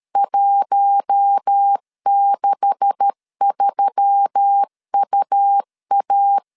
Morse code, heard as a keyed tone, about 13 wpm, 800 Hz. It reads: 163UA
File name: LEZIONI / 9 286 12wpm 2305